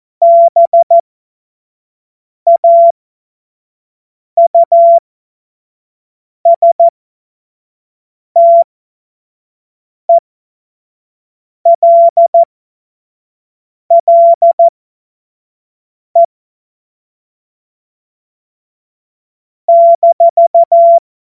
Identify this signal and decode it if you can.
Morse code, heard as a keyed tone, about 14 wpm, spaced as for 5 wpm, 680 Hz. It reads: BAUSTELLE -
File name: Morsekurs / Witze / Witz 10.Oktober 2023 25 F70